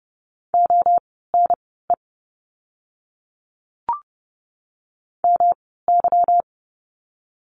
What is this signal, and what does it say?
Morse code, heard as a keyed tone, about 28 wpm, spaced as for 17 wpm, 700 Hz. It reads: ONE MY
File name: course / 30:10wpm / 237 Lesson 38f - Letter q - Words 30wpm